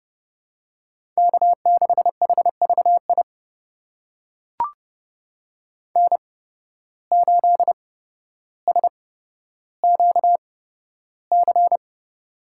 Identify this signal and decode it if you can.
Morse code, heard as a keyed tone, about 30 wpm, spaced as for 13 wpm, 700 Hz. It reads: K6HVI N8SQC
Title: USA Call Signs - 5 Characters - 8x Character Spacing - 30wpm